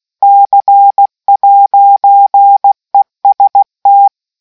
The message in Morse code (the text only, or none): C'EST